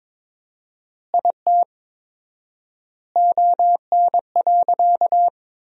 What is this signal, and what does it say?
Morse code, heard as a keyed tone, about 22 wpm, 700 Hz. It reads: IT ON.
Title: 4x Word-Spacing - Sentences from Top 200 Words - 22wpm